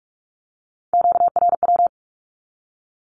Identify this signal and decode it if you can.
Morse code, heard as a keyed tone, about 46 wpm, 700 Hz. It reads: QRW
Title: Q-Codes with Spoken Meaning - 45wpm